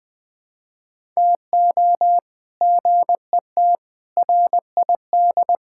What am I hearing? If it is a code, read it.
Morse code, heard as a keyed tone, about 21 wpm, 700 Hz. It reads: TO GET RID
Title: Build A Phrase - 5-Word Phrases - 20wpm